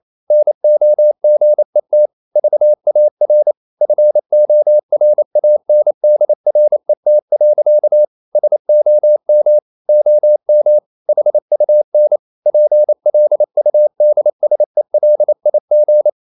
Morse code, heard as a keyed tone, about 28 wpm, 600 Hz. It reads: NOGET VAR FORANDRET. SOM OM HUN PLUDSELIG
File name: Ferie QRQ 28wpm